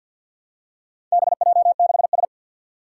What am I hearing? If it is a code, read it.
Morse code, heard as a keyed tone, about 50 wpm, 700 Hz. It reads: BJ6S
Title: Four Challenging Letter-Number Combinations 50wpm